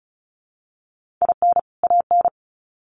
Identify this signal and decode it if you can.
Morse code, heard as a keyed tone, about 34 wpm, 700 Hz. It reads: IN AN
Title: Sets of 2 Words 35wpm